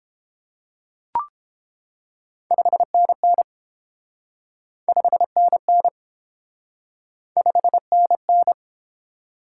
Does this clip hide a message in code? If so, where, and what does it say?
Yes: Morse code from 2.5 to 8.5 s, heard as a keyed tone, about 30 wpm, 700 Hz. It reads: 5NN 5NN 5NN